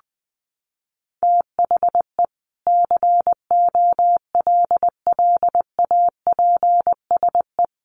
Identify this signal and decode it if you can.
Morse code, heard as a keyed tone, about 19 wpm, 700 Hz. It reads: THE COLLAPSE
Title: Sets of 5 Words Encore 20wpm